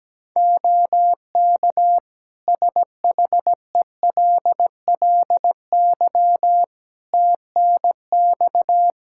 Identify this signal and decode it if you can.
Morse code, heard as a keyed tone, about 17 wpm, 700 Hz. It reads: OK SHELLY TNX
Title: Practice QSO - Exchange 4 - Ending - 17wpm